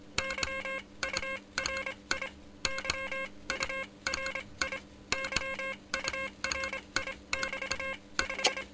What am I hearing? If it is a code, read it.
Morse code, heard as a keyed tone, about 26 wpm, spaced as for 21 wpm, 530 Hz, with noise. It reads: Y URI Y URI YURI<BT> 5
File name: Expedition 2018 / HUNT QSO s RX3PR